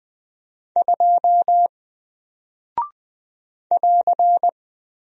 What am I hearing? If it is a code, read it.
Morse code, heard as a keyed tone, about 20 wpm, 700 Hz. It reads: 2 <AR>
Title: Single Character (Letter:Number:Punctuation:Prosign) Alt 20wpm